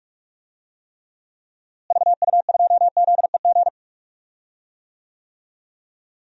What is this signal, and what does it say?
Morse code, heard as a keyed tone, about 46 wpm, 700 Hz. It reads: VU2ZEG
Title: DXCC-100 Call Signs - 6 Characters - Repeated 3x - 45wpm